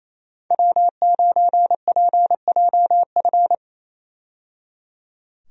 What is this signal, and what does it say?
Morse code, heard as a keyed tone, about 28 wpm, 700 Hz. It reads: W9PJF
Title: USA Call Signs - 5 Characters - 28wpm